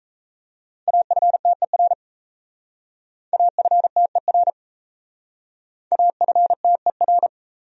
Morse code, heard as a keyed tone, about 37 wpm, 700 Hz. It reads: AFTER AFTER AFTER